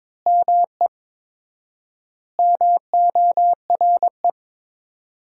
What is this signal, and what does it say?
Morse code, heard as a keyed tone, about 22 wpm, 700 Hz. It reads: ME MORE
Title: 4x Word-Spacing - Sentences from Top 200 Words - 22wpm